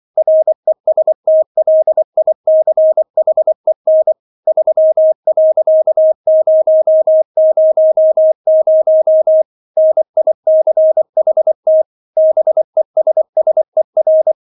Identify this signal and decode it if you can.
Morse code, heard as a keyed tone, about 24 wpm, 630 Hz. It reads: RESTLICHEN 3.000 NICHT BESSER